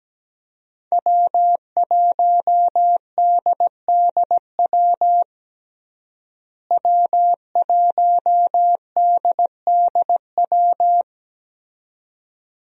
Morse code, heard as a keyed tone, about 17 wpm, 700 Hz. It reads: W1DDW W1DDW
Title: USA Call Signs - 5 Characters - Repeated 2x - 17wpm